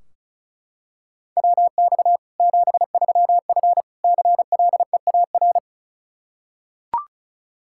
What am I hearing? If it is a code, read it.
Morse code, heard as a keyed tone, about 36 wpm, 700 Hz. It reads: WX 73F CLEAR
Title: Practice QSO - Exchange 2 - Info - 35wpm